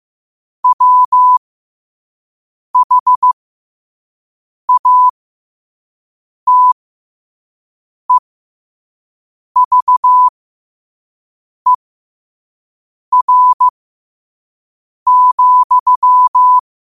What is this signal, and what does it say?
Morse code, heard as a keyed tone, about 15 wpm, spaced as for 5 wpm, 1000 Hz. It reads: WHATEVER,
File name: qotdfiles / 5wpm